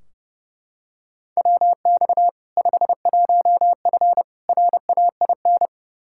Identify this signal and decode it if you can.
Morse code, heard as a keyed tone, about 30 wpm, 700 Hz. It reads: WX 51F RAIN
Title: K1USN SST QSO Practice 30wpm